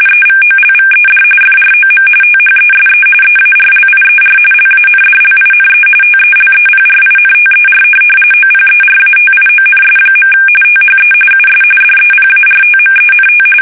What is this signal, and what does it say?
RTTY, about 45.45 baud. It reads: EVOLVED FROM THESE EARLIER LANDLINE TELEPRINTER OPERATIONS. THE US DEPARTMENT
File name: fsk850 / teletype-RTTY